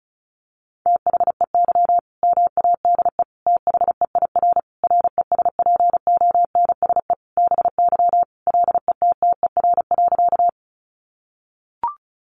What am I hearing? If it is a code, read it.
Morse code, heard as a keyed tone, about 34 wpm, 700 Hz. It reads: THEY MADE THEIR RESPONSE BY LETTER.
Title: Sentences from Top 1400 Words 35wpm